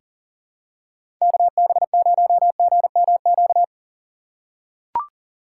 Morse code, heard as a keyed tone, about 39 wpm, 700 Hz. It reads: KB0GMQ
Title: USA Call Signs - 6 Characters - 4x Character Spacing - 40wpm